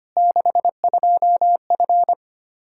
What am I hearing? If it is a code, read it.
Morse code, heard as a keyed tone, about 25 wpm, 700 Hz. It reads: 62F
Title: Practice QSO - Exchange 2 - Info - 25wpm